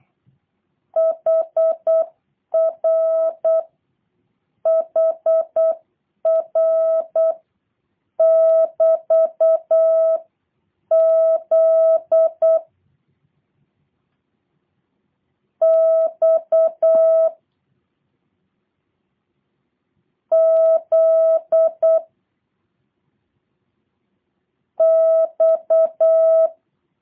Morse code, heard as a keyed tone, about 8 wpm, spaced as for 6 wpm, 650 Hz. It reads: HRHR<BT>Z X Z X